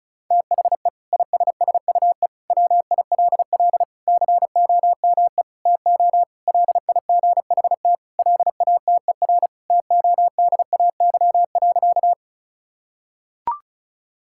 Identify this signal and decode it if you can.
Morse code, heard as a keyed tone, about 34 wpm, 700 Hz. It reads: THE ISSUE WILL COME TO LIGHT LATER TODAY.
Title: Sentences from Top 1000 Words - Review Part 6 - 35wpm